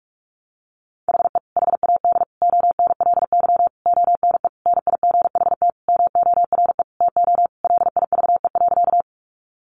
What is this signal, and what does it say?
Morse code, heard as a keyed tone, about 44 wpm, 700 Hz. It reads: HE HAD ONLY ONE NIGHT MORE TO LIVE.
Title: Sentences from Top 300 Words 45wpm